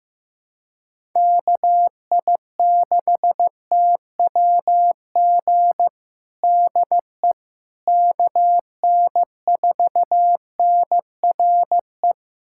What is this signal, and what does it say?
Morse code, heard as a keyed tone, about 16 wpm, 700 Hz. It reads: KI6TWG DE KN4NRE
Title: USA Callsign DE Callsign - 6 Characters - 15wpm